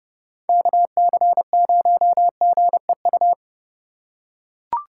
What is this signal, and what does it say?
Morse code, heard as a keyed tone, about 29 wpm, 700 Hz. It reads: KC0GEU